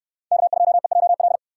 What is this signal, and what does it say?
Morse code, heard as a keyed tone, about 66 wpm, 700 Hz. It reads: K2EJL